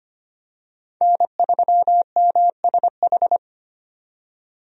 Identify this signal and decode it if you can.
Morse code, heard as a keyed tone, about 25 wpm, 700 Hz. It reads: N3MSH
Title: USA Call Signs - 5 Characters - 25wpm